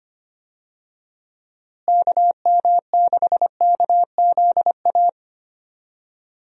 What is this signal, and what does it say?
Morse code, heard as a keyed tone, about 25 wpm, 700 Hz. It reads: KM6KZA
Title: USA Call Signs - 6 Characters - Repeated 2x - 25wpm